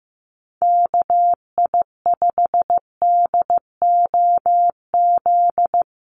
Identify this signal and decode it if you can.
Morse code, heard as a keyed tone, about 15 wpm, 700 Hz. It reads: KI5DOZ
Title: USA Call Signs 15wpm